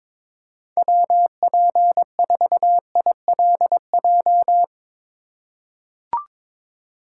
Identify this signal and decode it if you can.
Morse code, heard as a keyed tone, about 22 wpm, 700 Hz. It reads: WP4ILJ
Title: USA Call Signs - 6 Characters - 4x Character Spacing - 22wpm